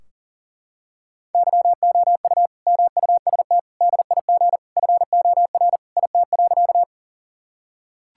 Morse code, heard as a keyed tone, about 40 wpm, 700 Hz. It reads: YOU MUST DIG FOR IT.
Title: Sentences from Top 1600 Words 40wpm